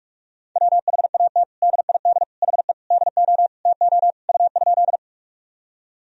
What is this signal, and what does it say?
Morse code, heard as a keyed tone, about 46 wpm, 700 Hz. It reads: WHAT DID HE DO TO U?
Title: Sentences from Top 100 Words and Abbreviations 45wpm